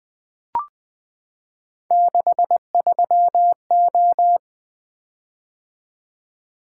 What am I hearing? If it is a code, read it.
Morse code, heard as a keyed tone, about 20 wpm, 700 Hz. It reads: 63O
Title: Three Letter-Number Combinations - 20wpm